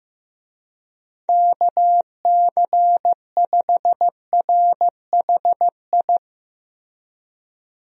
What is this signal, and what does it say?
Morse code, heard as a keyed tone, about 15 wpm, 700 Hz. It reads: KC5RHI
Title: USA Call Signs - 6 Characters - Repeated 3x - 15wpm